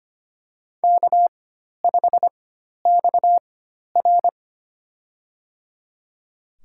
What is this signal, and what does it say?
Morse code, heard as a keyed tone, about 25 wpm, spaced as for 12 wpm, 700 Hz. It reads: K5XR
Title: USA Call Signs - 4 Characters - 4x Character Spacing - 25wpm